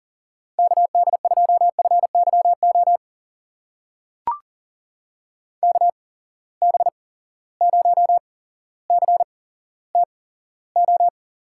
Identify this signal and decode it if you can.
Morse code, heard as a keyed tone, about 40 wpm, spaced as for 17 wpm, 700 Hz. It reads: KD2FYO KB0CTO
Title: USA Call Signs - 6 Characters - 8x Character Spacing - 40wpm